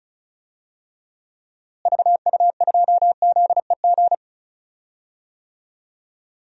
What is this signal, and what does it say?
Morse code, heard as a keyed tone, about 36 wpm, 700 Hz. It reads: VU2ZEG
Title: DXCC-100 Call Signs - 6 Characters - Repeated 3x - 35wpm